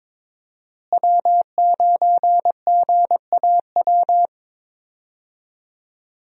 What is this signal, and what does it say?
Morse code, heard as a keyed tone, about 22 wpm, 700 Hz. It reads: W9GAW
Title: USA Call Signs - 5 Characters - Repeated 2x - 22wpm